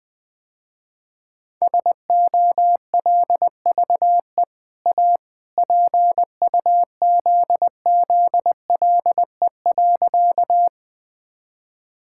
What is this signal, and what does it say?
Morse code, heard as a keyed tone, about 20 wpm, 700 Hz. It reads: SOLVE A PUZZLE.
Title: Easy Sentences with Hints - 20wpm